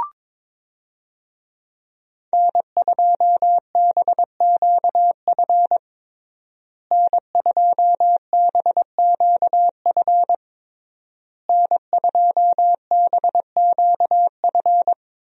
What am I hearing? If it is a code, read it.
Morse code, heard as a keyed tone, about 22 wpm, 700 Hz. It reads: N2BQF N2BQF N2BQF
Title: USA Call Signs - 5 Characters - Repeated 3x - 22wpm